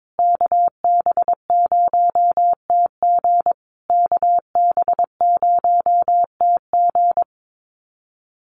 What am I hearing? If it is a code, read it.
Morse code, heard as a keyed tone, about 22 wpm, 700 Hz. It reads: KB0TG KB0TG
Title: USA Call Signs Repeated 3x 22wpm